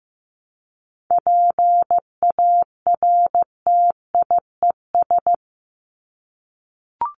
Morse code, heard as a keyed tone, about 15 wpm, 700 Hz. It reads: PARTIES
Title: ES:S Suffix - Top 100 Words with Base Words 15wpm